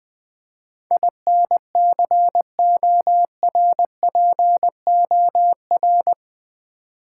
Morse code, heard as a keyed tone, about 20 wpm, 700 Hz. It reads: INCORPOR
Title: Build A Word - 4 Letters At A Time - 11-Letter Words - 20wpm